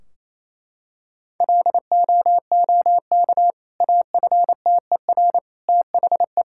Morse code, heard as a keyed tone, about 28 wpm, 700 Hz. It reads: LOOK AFTER THE
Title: Sentences from Top 1600 Words 28wpm